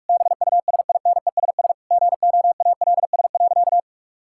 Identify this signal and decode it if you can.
Morse code, heard as a keyed tone, about 41 wpm, 700 Hz. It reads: BUSINESS GOALS.
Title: Sentences from Top 1000 Words - Review Part 6 - 45wpm